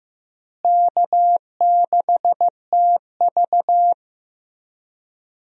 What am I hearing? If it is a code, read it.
Morse code, heard as a keyed tone, about 15 wpm, 700 Hz. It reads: K6TV
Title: USA Call Signs - 4 Characters - Repeated 3x - 15wpm